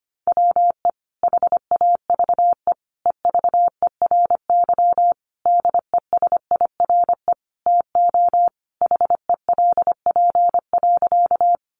Morse code, heard as a keyed tone, about 25 wpm, 700 Hz. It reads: WE HAVE EVERY DESIRE TO HELP.
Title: Sentences from Top 1100 Words 25wpm